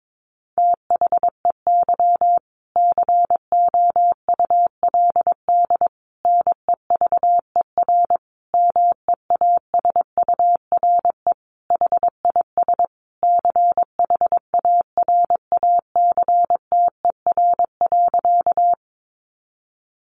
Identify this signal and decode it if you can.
Morse code, heard as a keyed tone, about 22 wpm, 700 Hz. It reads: THEY COULD NEVER MEASURE HIS CHARACTER.
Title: Sentences from Top 1200 Words 22wpm